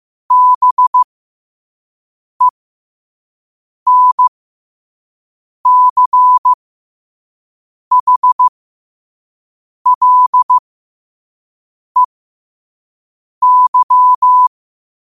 Morse code, heard as a keyed tone, about 15 wpm, spaced as for 5 wpm, 1000 Hz. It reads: BENCHLEY
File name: qotdfiles / 5wpm